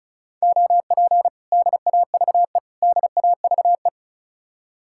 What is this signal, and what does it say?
Morse code, heard as a keyed tone, about 44 wpm, 700 Hz. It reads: OP DAVE DAVE